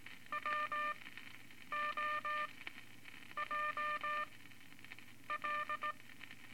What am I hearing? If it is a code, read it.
Morse code, heard as a keyed tone, about 19 wpm, spaced as for 8 wpm, 1200 Hz, with noise. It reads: WOJL
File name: Morse Code Records / 13A LCM 154 Units I-IV Groups of Four cont